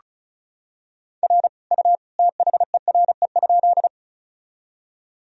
Morse code, heard as a keyed tone, about 34 wpm, 700 Hz. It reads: R U THERE?
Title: Sentences from Top 100 Words and Abbreviations 35wpm